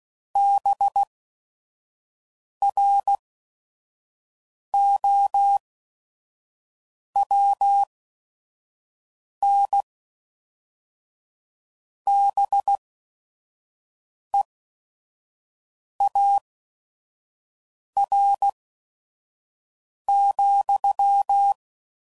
Morse code, heard as a keyed tone, about 16 wpm, spaced as for 5 wpm, 800 Hz. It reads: BROWNBEAR,